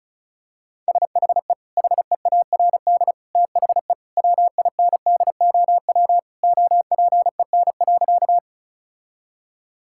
Morse code, heard as a keyed tone, about 36 wpm, 700 Hz. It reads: SHE HEARD THE WINDOW OPEN.
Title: Sentences from Top 1600 Words 35wpm